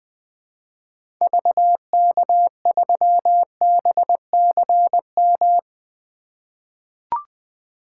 Morse code, heard as a keyed tone, about 20 wpm, 700 Hz. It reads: VK3BCM
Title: 2023 SOTA Activator Call Signs - 20wpm